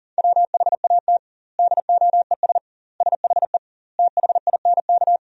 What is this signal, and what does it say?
Morse code, heard as a keyed tone, about 40 wpm, 700 Hz. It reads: WHAT DOES SHE THINK
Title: 2x Word-Spacing - Sentences from Top 200 Words - 40wpm